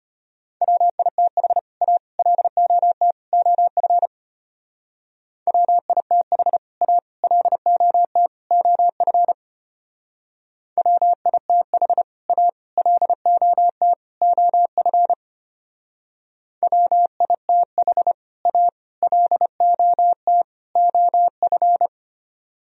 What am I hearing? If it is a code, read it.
Morse code, heard as a keyed tone, about 30 wpm, 700 Hz. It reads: WITH A LOT OF WITH A LOT OF WITH A LOT OF WITH A LOT OF
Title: Speed-Racing - Sets of 4 Words 25wpm